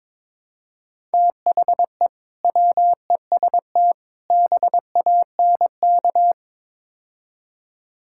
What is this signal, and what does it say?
Morse code, heard as a keyed tone, about 21 wpm, 700 Hz. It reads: THE WEST BANK
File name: practice / 22wpm / Build A Phrase - 5-Word Phrases - 22wpm